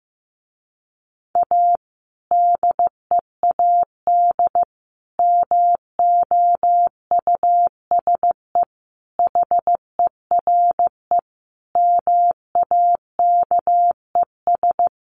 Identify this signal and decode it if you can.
Morse code, heard as a keyed tone, about 15 wpm, 700 Hz. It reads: A DEAD MOUSE HERE MAKES
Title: A Sound Of Thunder 15wpm